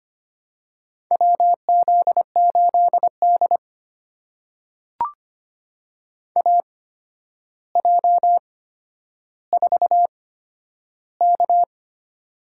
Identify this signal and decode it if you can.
Morse code, heard as a keyed tone, about 25 wpm, spaced as for 11 wpm, 700 Hz. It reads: WZ8D AJ4K
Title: USA Call Signs - 8x Character Spacing - 25wpm